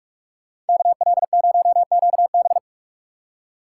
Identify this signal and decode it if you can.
Morse code, heard as a keyed tone, about 47 wpm, 690 Hz. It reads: KL0QB
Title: USA Call Signs - 5 Characters - 6x Character Spacing - 45wpm